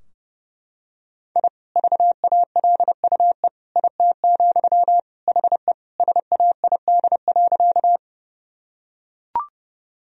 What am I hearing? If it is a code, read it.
Morse code, heard as a keyed tone, about 32 wpm, 700 Hz. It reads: I VALUE IT, HE SAID.
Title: Sentences from Top 1000 Words - Review Part 3 - 30wpm